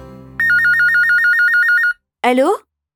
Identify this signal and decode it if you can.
Speech. There is no code.